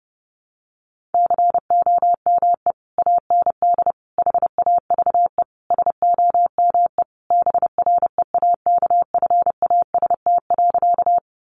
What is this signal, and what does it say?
Morse code, heard as a keyed tone, about 30 wpm, 700 Hz. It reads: COME AND HAVE SOME BREAKFAST.
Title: Sentences from Top 1500 Words 30wpm